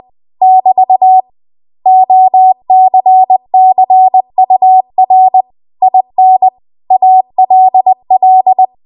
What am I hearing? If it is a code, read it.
Morse code, heard as a keyed tone, about 20 wpm, 750 Hz. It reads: <BT> OCCUR IN ALL
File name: Archive / 20 WPM / 20WPM